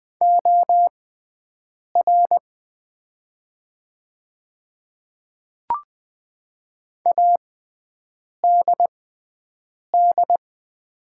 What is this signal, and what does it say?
Morse code, heard as a keyed tone, about 20 wpm, spaced as for 7 wpm, 690 Hz. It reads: OR ADD
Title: 6x Character Spacing - Top 200 Words - 20wpm